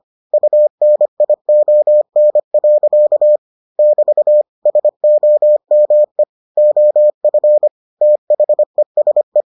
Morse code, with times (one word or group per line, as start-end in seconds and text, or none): 0.3-3.4 UNION.
3.8-4.4 <BT>
4.7-6.2 SOME
6.6-7.7 OF
8.0-9.4 THESE